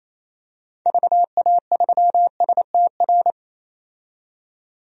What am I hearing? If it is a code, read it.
Morse code, heard as a keyed tone, about 28 wpm, 700 Hz. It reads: VA3STR
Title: DXCC-100 Call Signs - 6 Characters - 28wpm